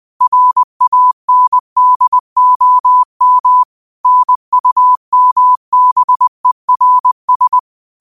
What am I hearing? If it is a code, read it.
Morse code, heard as a keyed tone, about 20 wpm, 1000 Hz. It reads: RANDOM NUMBERS
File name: qotdfiles / 20wpm